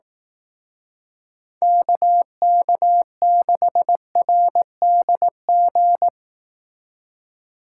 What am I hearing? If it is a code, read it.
Morse code, heard as a keyed tone, about 18 wpm, 700 Hz. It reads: KK6RDG
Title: Speed Racing - USA 2x3 Call Signs 15wpm